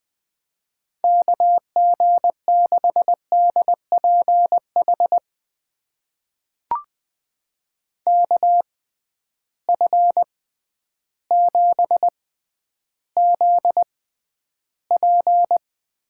Morse code, heard as a keyed tone, about 21 wpm, spaced as for 11 wpm, 700 Hz. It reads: KG6DPH KF7ZP
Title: USA Call Signs - 6 Characters - 6x Character Spacing - 20wpm